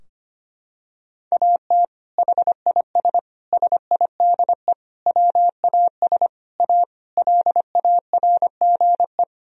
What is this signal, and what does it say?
Morse code, heard as a keyed tone, about 25 wpm, 700 Hz. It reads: AT HIS SIDE WAS A LARGE
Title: Sentences from Top 1000 Words - Review Part 2 - 25wpm